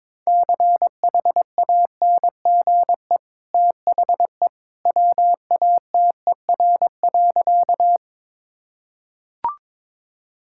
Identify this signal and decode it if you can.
Morse code, heard as a keyed tone, about 22 wpm, 700 Hz. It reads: CHANGE THE WATER.
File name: practice / 22wpm / Easy Sentences with Hints - 22wpm